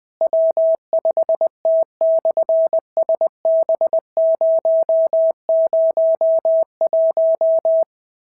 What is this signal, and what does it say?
Morse code, heard as a keyed tone, about 20 wpm, 650 Hz. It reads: W5T/SB001
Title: SOTA QSO Practice 20wpm